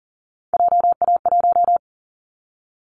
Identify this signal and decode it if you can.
Morse code, heard as a keyed tone, about 40 wpm, 700 Hz. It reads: JA1
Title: Callsign Prefixes for Top 50 DXCC - 40wpm